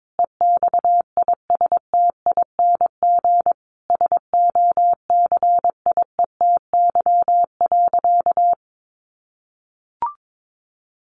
Morse code, heard as a keyed tone, about 22 wpm, 700 Hz. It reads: EXISTING SOCIETY.
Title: Sentences from Top 1500 Words 22wpm